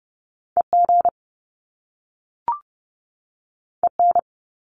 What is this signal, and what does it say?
Morse code, heard as a keyed tone, about 30 wpm, 700 Hz. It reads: EG EN
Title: Top Two Letter Combinations in Words 30wpm